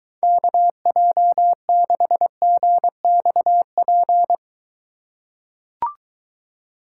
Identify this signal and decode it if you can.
Morse code, heard as a keyed tone, about 23 wpm, 700 Hz. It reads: KJ6GXP